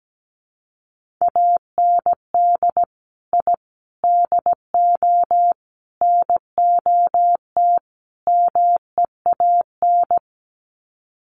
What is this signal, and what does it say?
Morse code, heard as a keyed tone, about 17 wpm, 700 Hz. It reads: AND I DO NOT MEAN
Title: Sets of 5 Words Encore2 17wpm